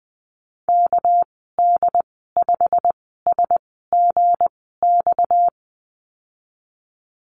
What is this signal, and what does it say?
Morse code, heard as a keyed tone, about 20 wpm, 700 Hz. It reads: K D 5 S G X
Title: 238 Lesson 38g - Letter q - Call Signs 20wpm